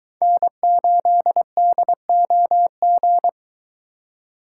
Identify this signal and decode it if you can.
Morse code, heard as a keyed tone, about 23 wpm, 700 Hz. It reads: N8DOG